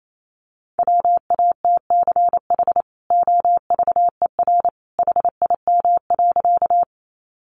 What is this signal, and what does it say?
Morse code, heard as a keyed tone, about 28 wpm, 700 Hz. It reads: WATCH OVER HIM.
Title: Sentences from Top 300 Words 28wpm